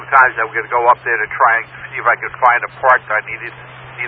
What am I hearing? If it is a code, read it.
Emergency Alert System (SAME) tones, heard as harsh buzzing bursts, with speech. It is heard from 0.3 to 3.4 s.